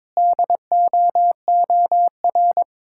Morse code, heard as a keyed tone, about 22 wpm, 700 Hz. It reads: DOOR
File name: practice / 22wpm / Sentences with 3-Word Phrases - 22wpm